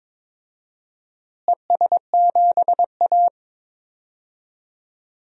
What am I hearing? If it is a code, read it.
Morse code, heard as a keyed tone, about 22 wpm, 700 Hz. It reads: ES7A
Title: DXCC-100 Call Signs - 4 Characters - Repeated 3x - 22wpm